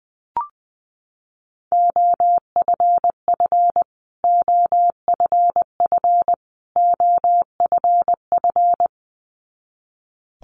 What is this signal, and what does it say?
Morse code, heard as a keyed tone, about 20 wpm, 700 Hz. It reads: OFF OFF OFF
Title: Top 500 Words Repeated 20wpm